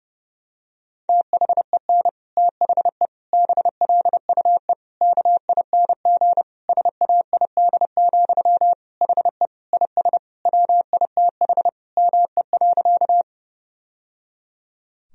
Morse code, heard as a keyed tone, about 30 wpm, 700 Hz. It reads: THEN THE BLUE KING SAID, HE IS WITH ME.
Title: Sentences from Top 1000 Words - Review Part 3 - 30wpm